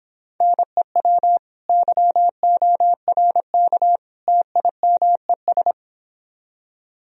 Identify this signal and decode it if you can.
Morse code, heard as a keyed tone, about 29 wpm, 700 Hz. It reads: NEW YORK TIMES